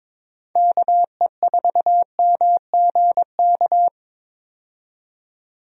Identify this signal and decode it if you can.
Morse code, heard as a keyed tone, about 21 wpm, 700 Hz. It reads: KE4MGK
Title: USA Call Signs - 6 Characters - 22wpm